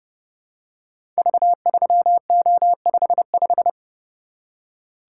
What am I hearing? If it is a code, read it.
Morse code, heard as a keyed tone, about 30 wpm, 700 Hz. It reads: V3O55
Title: Code Groups - 5 Letter:Numbers - 30wpm